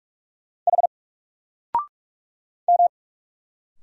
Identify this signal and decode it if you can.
Morse code, heard as a keyed tone, about 45 wpm, 700 Hz. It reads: H M
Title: Single Letter-Number 45wpm